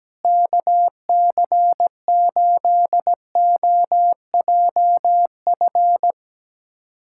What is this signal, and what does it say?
Morse code, heard as a keyed tone, about 17 wpm, 700 Hz. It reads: KC8OJF
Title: USA Callsign DE Callsign - 6 Characters - 17wpm